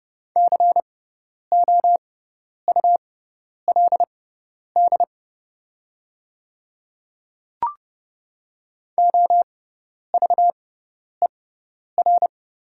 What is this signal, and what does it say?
Morse code, heard as a keyed tone, about 30 wpm, spaced as for 10 wpm, 690 Hz. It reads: COULD OVER
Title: 6x Character Spacing - Top 200 Words - 30wpm